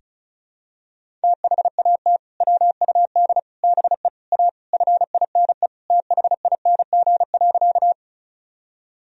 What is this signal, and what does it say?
Morse code, heard as a keyed tone, about 36 wpm, 700 Hz. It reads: THAT WUD BE A FINE THING.
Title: Sentences from Top 500 Words and Abbreviations 35wpm